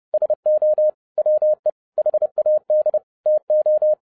Morse code, heard as a keyed tone, about 30 wpm, 610 Hz. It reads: SO WE HAD TO